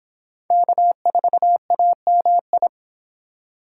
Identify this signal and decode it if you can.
Morse code, heard as a keyed tone, about 26 wpm, 700 Hz. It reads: K4AMI